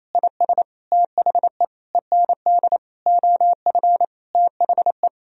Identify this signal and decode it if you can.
Morse code, heard as a keyed tone, about 28 wpm, 700 Hz. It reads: IS THE END OF THE
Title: Sentences from Top 1000 Words - Review Part 1 - 28wpm